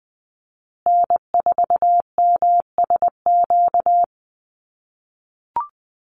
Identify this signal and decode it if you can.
Morse code, heard as a keyed tone, about 20 wpm, 700 Hz. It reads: N4MSQ
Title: USA Call Signs Repeated 2x 20wpm